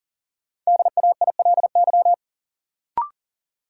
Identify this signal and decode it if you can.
Morse code, heard as a keyed tone, about 40 wpm, 700 Hz. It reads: DAILY
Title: Sentences from Top 1100 Words - New Word Drill - 40wpm